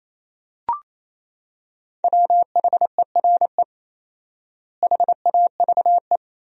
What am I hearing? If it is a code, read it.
Morse code, heard as a keyed tone, about 28 wpm, 700 Hz. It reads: WHERE HAVE
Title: 4x Word-Spacing - Sentences from Top 200 Words - 28wpm